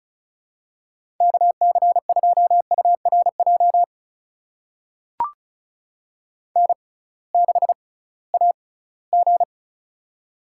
Morse code, heard as a keyed tone, about 34 wpm, spaced as for 20 wpm, 700 Hz. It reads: KC2URJ N6AG
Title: USA Call Signs - 6x Character Spacing - 35wpm